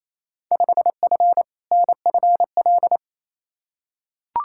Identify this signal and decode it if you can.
Morse code, heard as a keyed tone, about 28 wpm, 700 Hz. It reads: 5F NFL
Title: Field Day Exchanges Encore - 28wpm